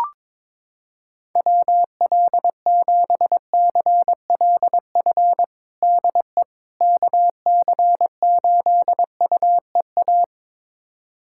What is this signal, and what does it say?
Morse code, heard as a keyed tone, about 22 wpm, 700 Hz. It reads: WL7CLF DE KC8UEA
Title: USA Callsign DE Callsign - 6 Characters - 22wpm